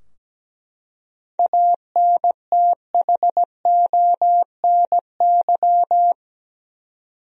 Morse code, heard as a keyed tone, about 17 wpm, 700 Hz. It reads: ANTHONY
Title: USA Common Names - 17wpm